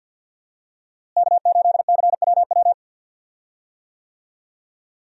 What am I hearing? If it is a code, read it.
Morse code, heard as a keyed tone, about 50 wpm, 700 Hz. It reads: K8CWW
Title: USA Call Signs - 5 Characters - Repeated 2x - 50wpm